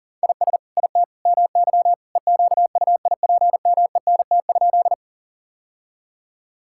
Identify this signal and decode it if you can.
Morse code, heard as a keyed tone, about 40 wpm, 700 Hz. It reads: IS IT MY EQUIPMENT?
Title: Sentences from Top 1000 Words - Review Part 5 - 40wpm